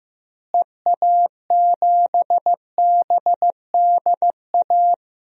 Morse code, heard as a keyed tone, about 15 wpm, 700 Hz. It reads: EA7BDA
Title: DXCC-100 Call Signs - 6 Characters - Repeated 2x - 15wpm